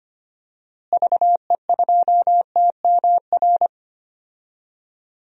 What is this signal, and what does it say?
Morse code, heard as a keyed tone, about 25 wpm, 700 Hz. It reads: VE2TMR
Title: DXCC-100 Call Signs - 6 Characters - Repeated 3x - 25wpm